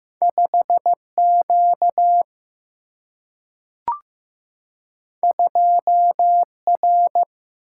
Morse code, heard as a keyed tone, about 15 wpm, 700 Hz. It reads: 5 Q 2 R
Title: Single-Letter Single-Number 15wpm